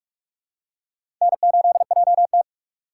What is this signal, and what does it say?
Morse code, heard as a keyed tone, about 42 wpm, 700 Hz. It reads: N8JT
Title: USA Call Signs - 3x Character Spacing - 45wpm